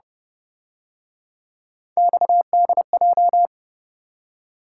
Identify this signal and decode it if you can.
Morse code, heard as a keyed tone, about 29 wpm, 700 Hz. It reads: XDJ